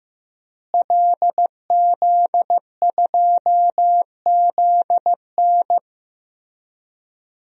Morse code, heard as a keyed tone, about 15 wpm, 700 Hz. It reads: LZ2ZN
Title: DXCC-100 Call Signs - 5 Characters - Repeated 3x - 15wpm